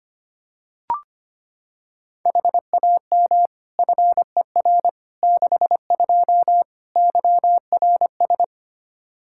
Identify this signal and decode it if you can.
Morse code, heard as a keyed tone, about 25 wpm, 700 Hz. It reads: HAM FER 62 YRS
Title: K1USN SST QSO Practice 25wpm